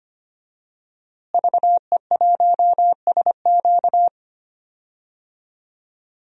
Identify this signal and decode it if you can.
Morse code, heard as a keyed tone, about 25 wpm, 700 Hz. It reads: VE1SQ
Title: DXCC-100 Call Signs - 5 Characters - Repeated 2x - 25wpm